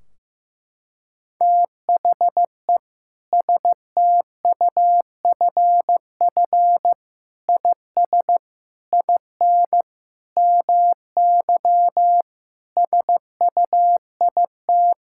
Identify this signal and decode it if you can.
Morse code, heard as a keyed tone, about 16 wpm, 700 Hz. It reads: THE STUFF IS IN MY SUIT
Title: Sentences from Top 1600 Words 15wpm